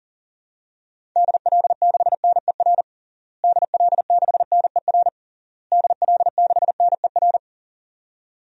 Morse code, heard as a keyed tone, about 40 wpm, 700 Hz. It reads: DL6NER DL6NER DL6NER
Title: DXCC-100 Call Signs - 6 Characters - Repeated 3x - 40wpm